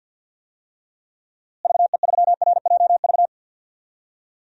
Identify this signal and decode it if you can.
Morse code, heard as a keyed tone, about 50 wpm, 700 Hz. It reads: VE3RJV
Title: DXCC-100 Call Signs - 6 Characters - Repeated 2x - 50wpm